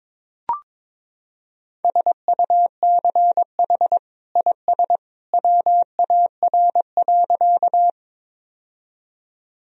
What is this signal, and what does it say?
Morse code, heard as a keyed tone, about 22 wpm, 700 Hz. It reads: SUCH IS WAR.
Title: Sentences from Top 1000 Words - Review Part 2 - 22wpm